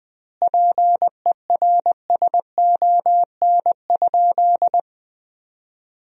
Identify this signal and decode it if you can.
Morse code, heard as a keyed tone, about 20 wpm, 700 Hz. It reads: PERSON?
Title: 2x Word-Spacing - Sentences from Top 400 Words - 20wpm